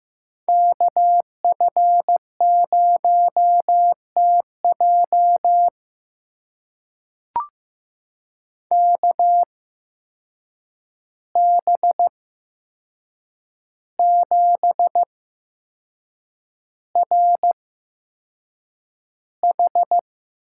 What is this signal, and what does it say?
Morse code, heard as a keyed tone, about 15 wpm, spaced as for 6 wpm, 700 Hz. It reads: KF0TJ KB7RH
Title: USA Call Signs - 5 Characters - 8x Character Spacing - 15wpm